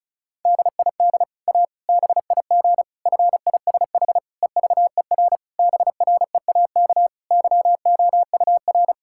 Morse code, heard as a keyed tone, about 36 wpm, 700 Hz. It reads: DID A BIG FISH EVER BREAK YOUR
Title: Sentences from Top 1000 Words - Review Part 6 - 35wpm